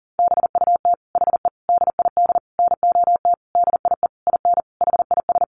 Morse code, heard as a keyed tone, about 40 wpm, 700 Hz. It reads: BUT HE DID NOT DIE IN HIS
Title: Sentences from Top 1300 Words 40wpm